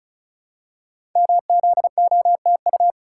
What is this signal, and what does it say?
Morse code, heard as a keyed tone, about 36 wpm, 700 Hz. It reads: MZOTU
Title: Code Groups - 5 Letters - 35wpm